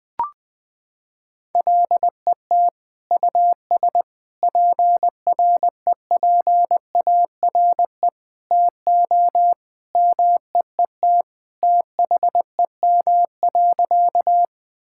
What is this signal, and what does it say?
Morse code, heard as a keyed tone, about 20 wpm, 700 Hz. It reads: LET US PREPARE TO MEET THEM.
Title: Sentences from Top 1000 Words - Review Part 6 - 20wpm